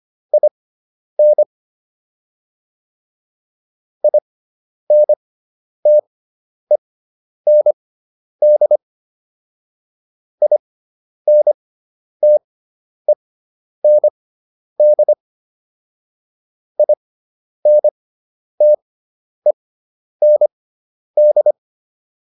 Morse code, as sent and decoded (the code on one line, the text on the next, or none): .. -. / .. -. - . -. -.. / .. -. - . -. -.. / .. -. - . -. -..
IN INTEND INTEND INTEND